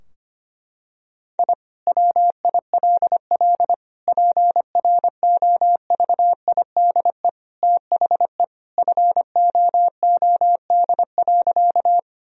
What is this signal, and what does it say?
Morse code, heard as a keyed tone, about 25 wpm, 700 Hz. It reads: I WILL PROVIDE THE FOOD.
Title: Sentences from Top 1000 Words - Review Part 3 - 25wpm